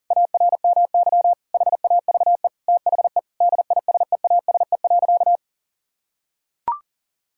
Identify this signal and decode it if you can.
Morse code, heard as a keyed tone, about 37 wpm, 700 Hz. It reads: ARMY HAVE THE DISEASE.
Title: Sentences from Top 1000 Words - Review Part 6 - 40wpm